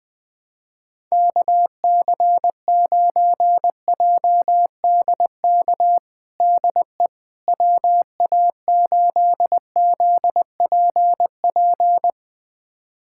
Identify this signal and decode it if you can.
Morse code, heard as a keyed tone, about 21 wpm, 700 Hz. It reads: KC9JDK DE WA8ZPP
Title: USA Callsign DE Callsign - 6 Characters - 20wpm